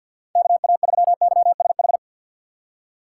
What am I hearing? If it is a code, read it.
Morse code, heard as a keyed tone, about 51 wpm, 700 Hz. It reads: KA3YSH
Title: USA Call Signs - 6 Characters - 3x Character Spacing - 50wpm